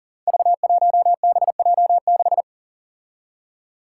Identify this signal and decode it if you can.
Morse code, heard as a keyed tone, about 40 wpm, 700 Hz. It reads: V1BJ6
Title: Five Challenging Letter-Number Combinations 40wpm